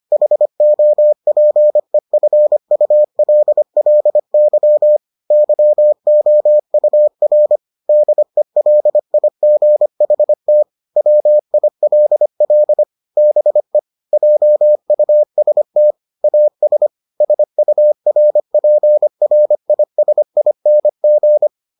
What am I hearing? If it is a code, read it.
Morse code, heard as a keyed tone, about 25 wpm, 600 Hz. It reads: HOPEFULLY YOUR DELIGHT WILL BE JUST AS SURPRISING